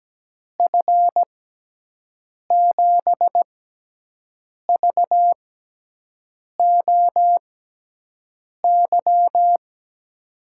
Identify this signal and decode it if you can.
Morse code, heard as a keyed tone, about 17 wpm, spaced as for 6 wpm, 700 Hz. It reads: F7VOY
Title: USA Call Signs - 6 Characters - 6x Character Spacing - 17wpm